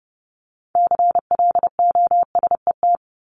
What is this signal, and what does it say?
Morse code, heard as a keyed tone, about 30 wpm, 700 Hz. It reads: CLOSET
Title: Top 1751-2000 Words 30wpm